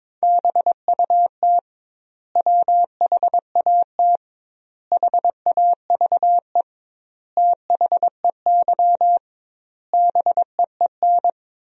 Morse code, heard as a keyed tone, about 22 wpm, 700 Hz. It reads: BUT WHAT HAVE THEY BEEN
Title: 2x Word-Spacing - Sentences from Top 100 Words - 22wpm